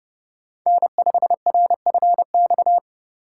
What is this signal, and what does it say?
Morse code, heard as a keyed tone, about 30 wpm, 700 Hz. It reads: N5RFX
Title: USA Call Signs - 5 Characters - 3x Character Spacing - 30wpm